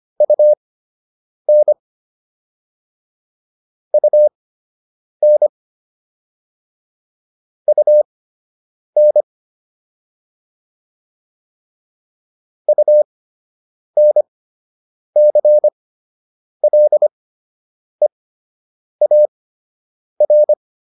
Morse code, heard as a keyed tone, about 25 wpm, spaced as for 8 wpm, 600 Hz. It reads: UN UN UN UNCLEAR